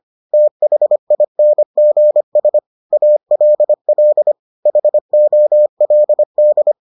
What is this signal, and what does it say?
Morse code, heard as a keyed tone, about 25 wpm, 600 Hz. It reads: THINGS ALL HOLD